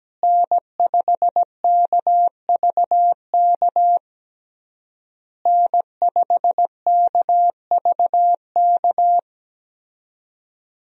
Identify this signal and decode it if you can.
Morse code, heard as a keyed tone, about 17 wpm, 700 Hz. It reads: N5KVK N5KVK
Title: USA Call Signs - 5 Characters - Repeated 3x - 17wpm